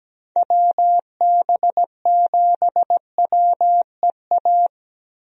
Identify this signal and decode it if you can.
Morse code, heard as a keyed tone, about 17 wpm, 700 Hz. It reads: WB7WEA
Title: USA Call Signs - 6 Characters - Repeated 2x - 17wpm